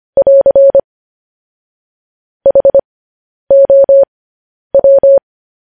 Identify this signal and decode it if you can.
Morse code, heard as a keyed tone, about 25 wpm, spaced as for 10 wpm, 550 Hz. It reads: <AR> HOW
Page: News Headlines in Morse Code at 10 WPM